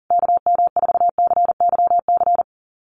Morse code, heard as a keyed tone, about 37 wpm, 700 Hz. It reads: KM4CYC
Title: USA Call Signs Repeated 3x 40wpm